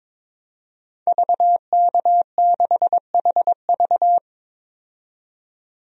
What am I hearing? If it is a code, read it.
Morse code, heard as a keyed tone, about 22 wpm, 700 Hz. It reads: VK6HV
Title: DXCC-100 Call Signs - 5 Characters - 22wpm